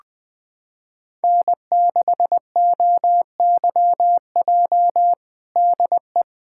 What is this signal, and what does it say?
Morse code, heard as a keyed tone, about 21 wpm, 700 Hz. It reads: N6OYJ DE
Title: USA Callsign DE Callsign - 5 Characters - 20wpm